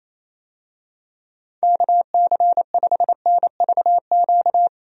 Morse code, heard as a keyed tone, about 28 wpm, 700 Hz. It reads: KC5NVQ
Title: USA Call Signs - 6 Characters - Repeated 2x - 28wpm